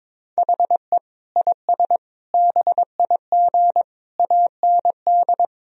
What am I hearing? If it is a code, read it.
Morse code, heard as a keyed tone, about 22 wpm, 700 Hz. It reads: HE IS BIG AND
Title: Sentences from Top 1000 Words - Review Part 3 - 22wpm